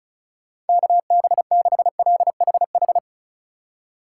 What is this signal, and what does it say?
Morse code, heard as a keyed tone, about 36 wpm, 700 Hz. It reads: KB6LHH
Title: USA Call Signs - 6 Characters - 2x Character Spacing - 35wpm